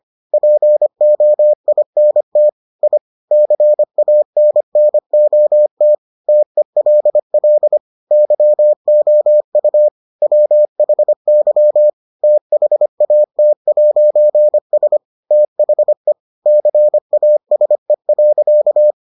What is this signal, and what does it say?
Morse code, heard as a keyed tone, about 25 wpm, 600 Hz. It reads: POINT I CANNOT TELL YOU WHY THAT'S THE CASE.